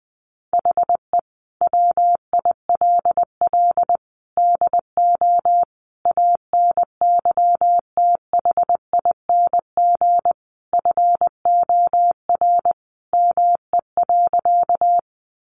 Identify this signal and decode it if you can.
Morse code, heard as a keyed tone, about 20 wpm, 700 Hz. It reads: HE WILL DO ANYTHING FOR ME.
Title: Sentences from Top 1000 Words 20wpm